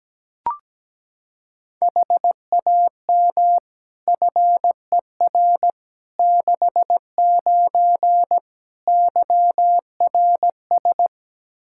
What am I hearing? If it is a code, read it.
Morse code, heard as a keyed tone, about 17 wpm, 700 Hz. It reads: HAM FER 69 YRS
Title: K1USN SST QSO Practice 17wpm